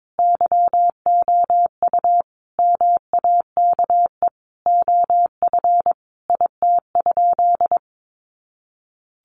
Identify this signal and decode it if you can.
Morse code, heard as a keyed tone, about 22 wpm, 700 Hz. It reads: YOU MAKE OF IT?
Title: Sentences from Top 100 Words 22wpm